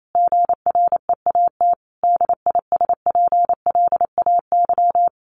Morse code, heard as a keyed tone, about 28 wpm, 700 Hz. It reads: GREAT DISPLAY
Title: Sentences from Top 1200 Words 28wpm